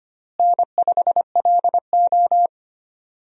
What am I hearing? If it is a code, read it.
Morse code, heard as a keyed tone, about 25 wpm, 700 Hz. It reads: N5LO
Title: USA Call Signs - 4 Characters - Repeated 2x - 25wpm